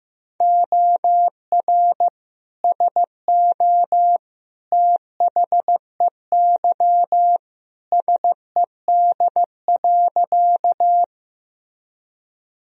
Morse code, heard as a keyed tone, about 15 wpm, 700 Hz. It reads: OR SO THEY SED.
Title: Sentences from Top 100 Words and Abbreviations 15wpm